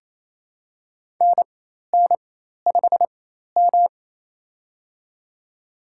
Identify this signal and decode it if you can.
Morse code, heard as a keyed tone, about 28 wpm, spaced as for 13 wpm, 700 Hz. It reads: NN5M
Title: USA Call Signs - 4 Characters - 4x Character Spacing - 28wpm